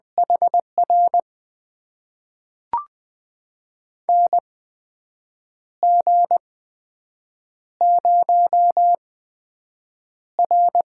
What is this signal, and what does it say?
Morse code, heard as a keyed tone, about 20 wpm, spaced as for 7 wpm, 700 Hz. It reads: HR NG0R
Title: USA Call Signs - 8x Character Spacing - 20wpm